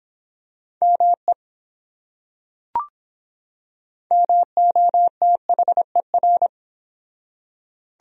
Morse code, heard as a keyed tone, about 24 wpm, 700 Hz. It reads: ME MOTHER